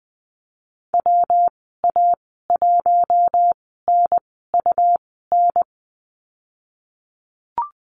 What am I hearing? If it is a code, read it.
Morse code, heard as a keyed tone, about 20 wpm, spaced as for 15 wpm, 700 Hz. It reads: WA1NUN
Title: 161 Lesson 26g - Letter g - Call Signs 20wpm